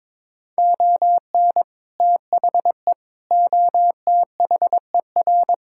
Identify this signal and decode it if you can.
Morse code, heard as a keyed tone, about 22 wpm, 700 Hz. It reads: ON THE OTHER
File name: practice / 22wpm / Sets of 7 Words - 22wpm